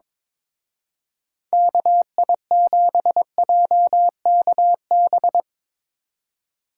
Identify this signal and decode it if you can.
Morse code, heard as a keyed tone, about 23 wpm, 700 Hz. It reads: KI7JKB
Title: Speed Racing - USA 2x3 Call Signs 22wpm